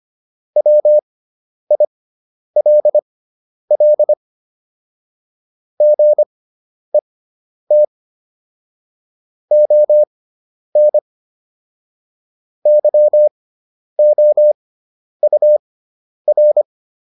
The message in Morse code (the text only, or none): WILL GET ON YOUR